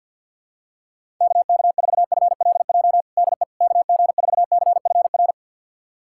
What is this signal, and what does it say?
Morse code, heard as a keyed tone, about 51 wpm, 700 Hz. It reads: KK4FLJ DE KG4CFR
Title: USA Callsign DE Callsign - 6 Characters - 50wpm